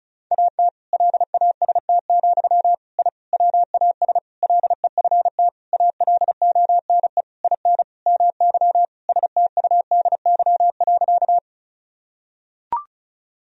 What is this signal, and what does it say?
Morse code, heard as a keyed tone, about 36 wpm, 700 Hz. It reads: AT LAST, I WAS LEFT ALONE IN MY STUDY.
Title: Sentences from Top 1000 Words - Review Part 1 - 35wpm